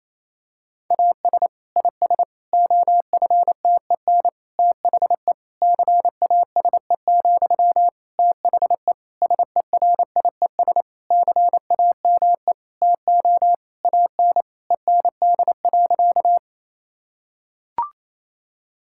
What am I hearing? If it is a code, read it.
Morse code, heard as a keyed tone, about 27 wpm, 700 Hz. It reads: AS IS OFTEN THE CASE, THE SERIES CAME TO AN END.
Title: Sentences from Top 1000 Words - Review Part 5 - 28wpm